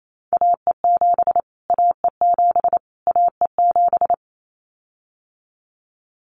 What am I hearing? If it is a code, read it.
Morse code, heard as a keyed tone, about 28 wpm, 700 Hz. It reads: AE7 AE7 AE7
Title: Callsign Prefixes for Top 25 DXCC - Repeated 3x - 28wpm